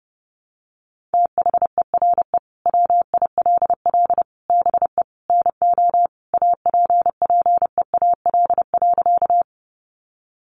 Morse code, heard as a keyed tone, about 28 wpm, 700 Hz. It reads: THERE WILL BE NO APPEAL.
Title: Sentences from Top 1300 Words 30wpm